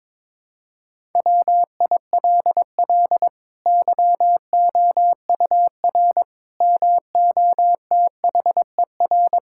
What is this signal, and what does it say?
Morse code, heard as a keyed tone, about 22 wpm, 700 Hz. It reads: WILL YOUR MOTHER
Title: Sentences from Top 1000 Words - Review Part 5 - 22wpm